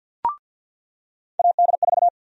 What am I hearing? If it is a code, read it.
Morse code, heard as a keyed tone, about 47 wpm, 700 Hz. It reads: AD4
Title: USA Call Sign Prefixes - 50wpm